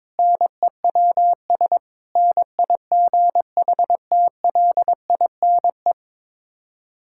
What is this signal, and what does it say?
Morse code, heard as a keyed tone, about 22 wpm, 700 Hz. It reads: NEWS NIGHTLINE
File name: practice / 22wpm / Build A Phrase - 5-Word Phrases - 22wpm